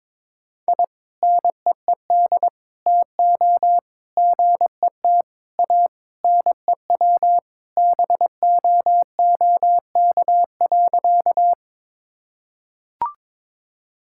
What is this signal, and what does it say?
Morse code, heard as a keyed tone, about 22 wpm, 700 Hz. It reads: I NEED TO GET A NEW BOOK.
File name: practice / 22wpm / Sentences with 3-Word Phrases - 22wpm